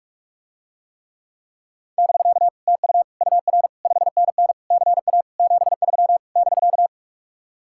Morse code, heard as a keyed tone, about 45 wpm, 700 Hz. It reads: <BK> TU UR 5NN CA 73 <BK>